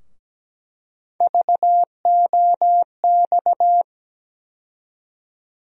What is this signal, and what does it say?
Morse code, heard as a keyed tone, about 17 wpm, 700 Hz. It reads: VOX
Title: USA Call Sign Postfixes - 3 Letters 17wpm